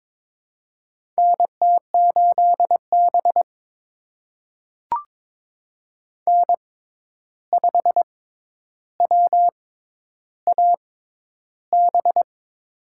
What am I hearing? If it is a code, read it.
Morse code, heard as a keyed tone, about 22 wpm, spaced as for 10 wpm, 700 Hz. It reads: NT8B N5WAB
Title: USA Call Signs - 6x Character Spacing - 22wpm